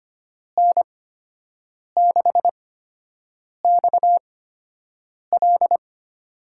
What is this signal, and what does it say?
Morse code, heard as a keyed tone, about 25 wpm, spaced as for 7 wpm, 700 Hz. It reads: N6XL
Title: USA Call Signs - 5 Characters - 8x Character Spacing - 25wpm